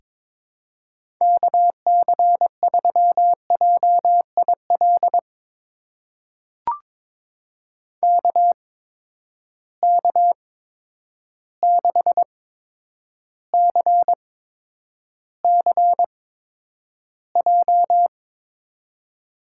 Morse code, heard as a keyed tone, about 22 wpm, spaced as for 9 wpm, 700 Hz. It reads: KC3JIL KK6CCJ
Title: USA Call Signs - 6 Characters - 8x Character Spacing - 22wpm